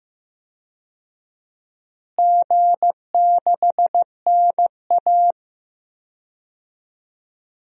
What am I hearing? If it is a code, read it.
Morse code, heard as a keyed tone, about 15 wpm, 700 Hz. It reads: G6NA
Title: DXCC-100 Call Signs - 4 Characters - Repeated 2x - 15wpm